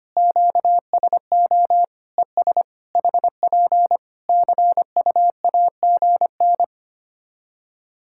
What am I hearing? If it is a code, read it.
Morse code, heard as a keyed tone, about 25 wpm, 700 Hz. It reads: QSO ES HP CUAGN
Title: Practice QSO - Exchange 4 - Ending - 25wpm